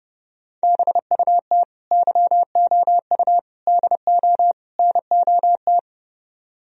Morse code, heard as a keyed tone, about 30 wpm, 700 Hz. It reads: BUT YOU DO NOT